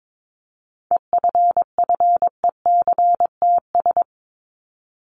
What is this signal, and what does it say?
Morse code, heard as a keyed tone, about 22 wpm, 700 Hz. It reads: EFFECTS
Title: ES:S Suffix - Top 100 Words with Base Words 22wpm